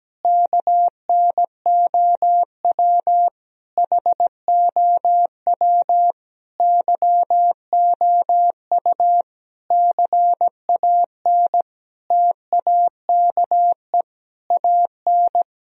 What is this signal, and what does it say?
Morse code, heard as a keyed tone, about 17 wpm, 700 Hz. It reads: KNOW HOW YOU CAN TAKE AN
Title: Sentences from Top 1000 Words - Review Part 2 - 17wpm